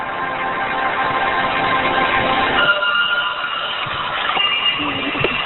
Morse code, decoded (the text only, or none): none